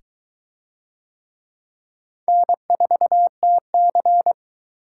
Morse code, heard as a keyed tone, about 22 wpm, 700 Hz. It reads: N4TC